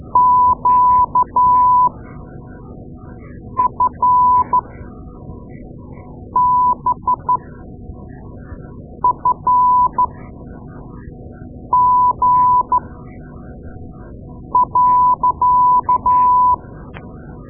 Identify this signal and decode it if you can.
Morse code, heard as a keyed tone, about 13 wpm, spaced as for 4 wpm, 1000 Hz. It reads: QFBFG.
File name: Morse Lessons / Morse Lessons / Tape-2A